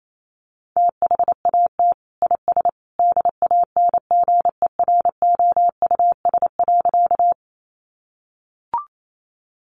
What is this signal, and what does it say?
Morse code, heard as a keyed tone, about 28 wpm, 700 Hz. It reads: THAT IS DANGEROUS.
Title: Sentences from Top 1200 Words 28wpm